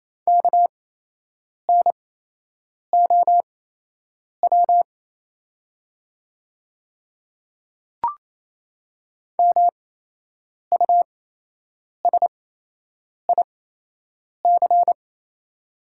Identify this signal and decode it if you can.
Morse code, heard as a keyed tone, about 28 wpm, spaced as for 8 wpm, 700 Hz. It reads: KNOW MUSIC
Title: 8x Character Spacing - Top 500 Words - 28wpm